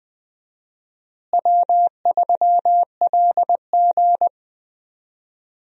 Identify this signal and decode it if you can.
Morse code, heard as a keyed tone, about 20 wpm, 700 Hz. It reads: W3LG
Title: USA Call Signs - 4 Characters - Repeated 3x - 20wpm